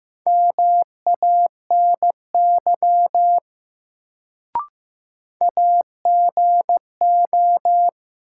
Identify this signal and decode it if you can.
Morse code, heard as a keyed tone, about 15 wpm, 700 Hz. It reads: MANY AGO
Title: Sets of 2 Words - New Word Drill - 15wpm